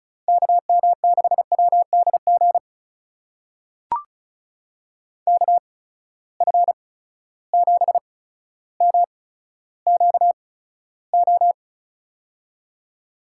Morse code, heard as a keyed tone, about 36 wpm, spaced as for 15 wpm, 700 Hz. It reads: KM6WDG KF7MQO
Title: USA Call Signs - 6 Characters - 8x Character Spacing - 35wpm